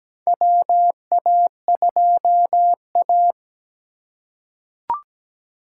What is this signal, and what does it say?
Morse code, heard as a keyed tone, about 17 wpm, 700 Hz. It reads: WA2A
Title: USA Call Signs - 3x Character Spacing - 17wpm